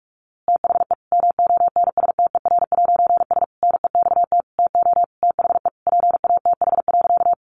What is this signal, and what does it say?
Morse code, heard as a keyed tone, about 46 wpm, 700 Hz. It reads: THE MONSTER'S NEXT TO THE PATH.
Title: A Sound Of Thunder 45wpm